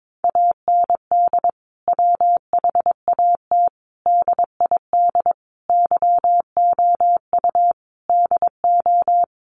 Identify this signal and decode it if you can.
Morse code, heard as a keyed tone, about 22 wpm, 700 Hz. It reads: AND WHAT DID YOU DO
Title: Sentences from Top 100 Words 22wpm